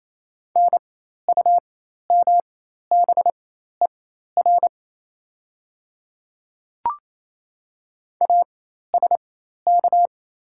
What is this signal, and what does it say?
Morse code, heard as a keyed tone, about 27 wpm, spaced as for 13 wpm, 690 Hz. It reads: NUMBER ASK
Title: 4x Character Spacing - Top 300 Words - 28wpm